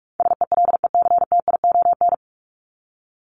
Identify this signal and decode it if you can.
Morse code, heard as a keyed tone, about 44 wpm, 700 Hz. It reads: SELECTION
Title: Top 1251-1500 Words 45wpm